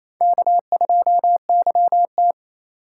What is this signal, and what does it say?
Morse code, heard as a keyed tone, about 34 wpm, 700 Hz. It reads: K2YT
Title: Speed Racing - USA 1x2 Call Signs 28wpm